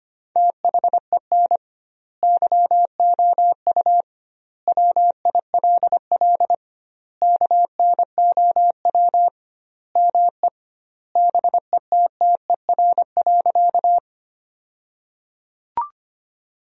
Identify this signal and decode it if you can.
Morse code, heard as a keyed tone, about 25 wpm, 700 Hz. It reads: THEN YOU WILL KNOW ME BETTER.
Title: 2x Word-Spacing - Sentences from Top 400 Words - 25wpm